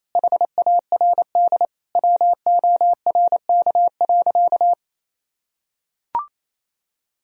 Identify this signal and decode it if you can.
Morse code, heard as a keyed tone, about 28 wpm, 700 Hz. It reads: HARD WORK.
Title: Sentences from Top 1000 Words - Review Part 3 - 28wpm